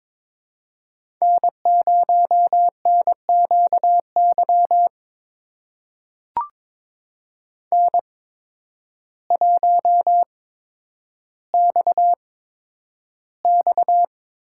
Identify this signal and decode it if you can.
Morse code, heard as a keyed tone, about 23 wpm, spaced as for 10 wpm, 700 Hz. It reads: N0NQY N1XX
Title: USA Call Signs - 5 Characters - 8x Character Spacing - 22wpm